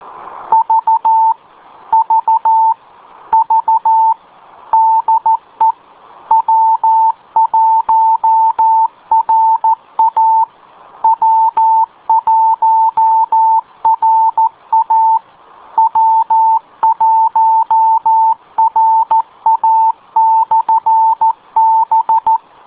Morse code, heard as a keyed tone, about 14 wpm, 890 Hz. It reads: V V V DE W1RA W1RA W1RA/B